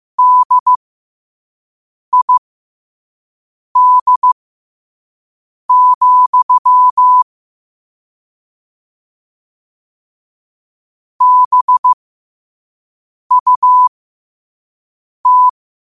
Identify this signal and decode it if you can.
Morse code, heard as a keyed tone, about 15 wpm, spaced as for 5 wpm, 1000 Hz. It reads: DID, BUT
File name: qotdfiles / 5wpm